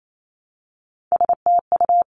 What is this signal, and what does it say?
Morse code, heard as a keyed tone, about 28 wpm, 700 Hz. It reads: STU
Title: Three Challenging Letter-Number Combinations 28wpm